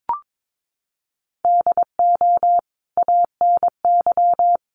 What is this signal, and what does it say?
Morse code, heard as a keyed tone, about 22 wpm, 700 Hz. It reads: DO ANY
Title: Sentences from Top 1100 Words 22wpm